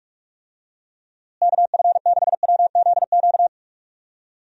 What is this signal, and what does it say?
Morse code, heard as a keyed tone, about 41 wpm, 700 Hz. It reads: KF6WZQ
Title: USA Call Signs - 6 Characters - Repeated 3x - 45wpm